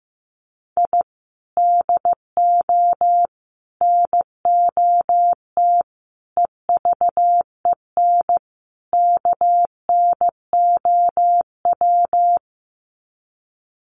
Morse code, heard as a keyed tone, about 15 wpm, 700 Hz. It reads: I DO NOT EVEN KNOW
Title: Sets of 5 Words 15wpm